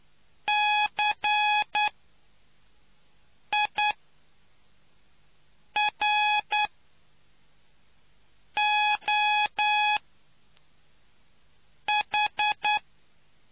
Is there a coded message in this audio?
Yes: Morse code from 0.5 to 12.8 s, heard as a keyed tone, about 9 wpm, spaced as for 4 wpm, 840 Hz. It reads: CIROH